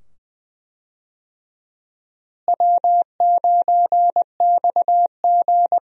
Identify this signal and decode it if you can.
Morse code, heard as a keyed tone, about 20 wpm, 700 Hz. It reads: W9XG
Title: USA Call Signs - 4 Characters - Repeated 2x - 20wpm